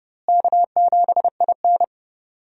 Morse code, heard as a keyed tone, about 30 wpm, 700 Hz. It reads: K7IN
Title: USA Call Signs - 4 Characters - Repeated 2x - 30wpm